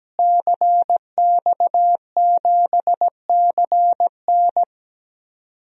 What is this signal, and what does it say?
Morse code, heard as a keyed tone, about 17 wpm, 700 Hz. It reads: CX7CN
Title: DXCC-100 Call Signs - 5 Characters - Repeated 2x - 17wpm